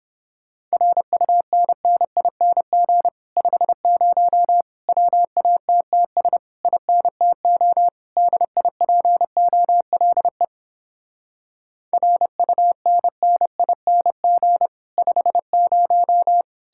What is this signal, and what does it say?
Morse code, heard as a keyed tone, about 31 wpm, 700 Hz. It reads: RUNNING 50 WATTS INTO DIPOLE RUNNING 50